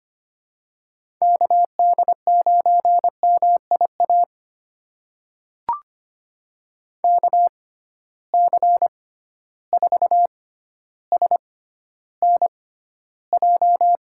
Morse code, heard as a keyed tone, about 25 wpm, spaced as for 13 wpm, 700 Hz. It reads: KD9MIA KC4SNJ
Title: USA Call Signs - 6 Characters - 6x Character Spacing - 25wpm